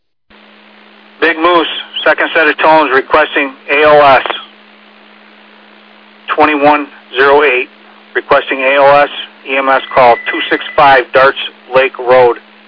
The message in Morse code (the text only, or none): none